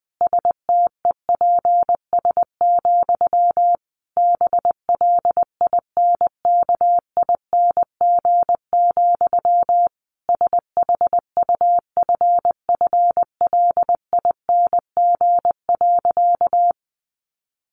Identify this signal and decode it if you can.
Morse code, heard as a keyed tone, about 20 wpm, 700 Hz. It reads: STEPS, BLINKING, SHUFFLING.
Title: A Sound Of Thunder 20wpm